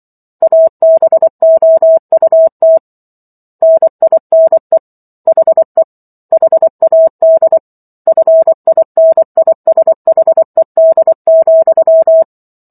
Morse code, heard as a keyed tone, about 24 wpm, 650 Hz. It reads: ABOUT NINE HE HAD FINISHED,
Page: War of the Worlds - 15-Chapter 15 - 24 WPM